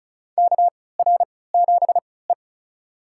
Morse code, heard as a keyed tone, about 36 wpm, spaced as for 20 wpm, 700 Hz. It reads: KR7E
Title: USA Call Signs - 4 Characters - 3x Character Spacing - 35wpm